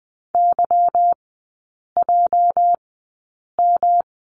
Morse code, heard as a keyed tone, about 20 wpm, spaced as for 8 wpm, 700 Hz. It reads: YJM
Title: 205 Lesson 33e - Letter j - Three Characters 20wpm